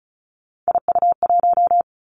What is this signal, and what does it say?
Morse code, heard as a keyed tone, about 36 wpm, 700 Hz. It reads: IU1
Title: Three Challenging Letter-Number Combinations 35wpm